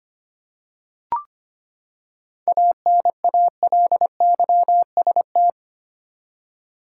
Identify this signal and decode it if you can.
Morse code, heard as a keyed tone, about 25 wpm, 700 Hz. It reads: ANALYST
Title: Seven Letter Words - 25wpm